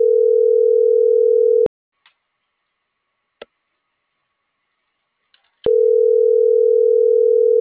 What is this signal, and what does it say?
Ringback tone (North America), 440 + 480 Hz.